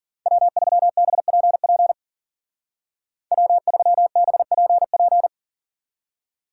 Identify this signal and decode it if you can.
Morse code, heard as a keyed tone, about 50 wpm, 700 Hz. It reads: W3BPP W3BPP